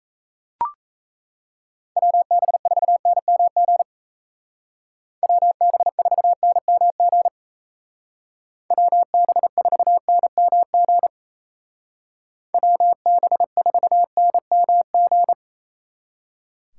Morse code, heard as a keyed tone, about 36 wpm, 700 Hz. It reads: WB4NMG WB4NMG WB4NMG WB4NMG